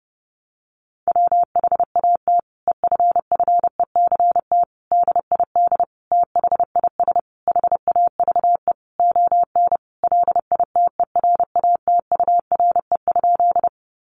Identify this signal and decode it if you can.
Morse code, heard as a keyed tone, about 30 wpm, 700 Hz. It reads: WHAT EFFECT DID THIS HAVE ON LITERATURE?
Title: Sentences from Top 1000 Words 30wpm